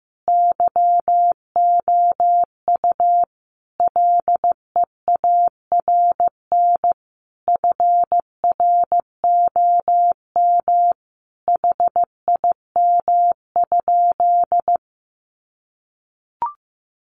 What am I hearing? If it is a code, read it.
Morse code, heard as a keyed tone, about 15 wpm, 700 Hz. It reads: YOU LEARN FROM HIM?
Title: Sentences from Top 300 Words 15wpm